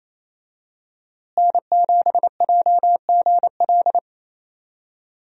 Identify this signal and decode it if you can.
Morse code, heard as a keyed tone, about 28 wpm, 700 Hz. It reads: N7JGL
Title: USA Call Signs - 5 Characters - Repeated 3x - 28wpm